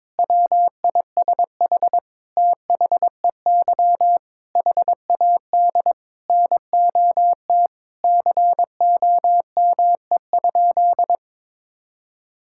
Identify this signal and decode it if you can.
Morse code, heard as a keyed tone, about 22 wpm, 700 Hz. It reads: WISH THEY HAD NOT COME?
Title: Sentences from Top 1600 Words 22wpm